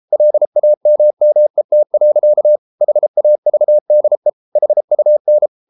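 Morse code, heard as a keyed tone, about 33 wpm, 600 Hz. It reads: LAMMET. HAVDE HUN
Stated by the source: Morse code at 33 wpm, QRQ.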